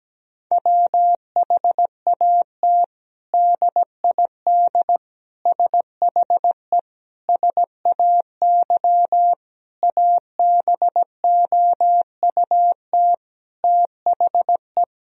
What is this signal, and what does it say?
Morse code, heard as a keyed tone, about 17 wpm, 700 Hz. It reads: WHAT DID SHE SAY ABOUT THE
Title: Sentences from Top 1000 Words - Review Part 2 - 17wpm